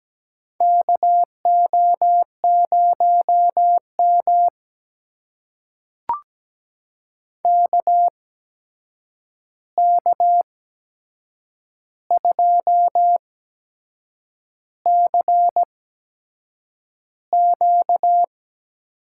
Morse code, heard as a keyed tone, about 17 wpm, spaced as for 7 wpm, 700 Hz. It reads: KO0M KK2CQ
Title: USA Call Signs - 8x Character Spacing - 17wpm